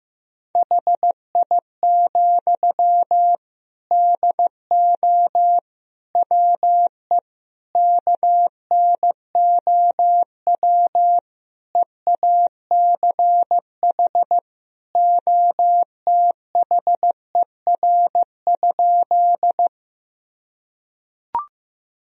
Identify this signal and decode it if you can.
Morse code, heard as a keyed tone, about 15 wpm, 700 Hz. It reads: HI, DO WE KNOW EACH OTHER?
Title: Sentences from Top 1000 Words - Review Part 3 - 15wpm